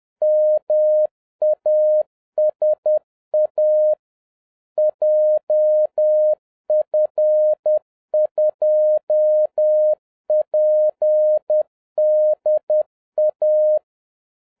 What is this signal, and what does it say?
Morse code, heard as a keyed tone, about 10 wpm, 610 Hz. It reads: MASA JF2PDA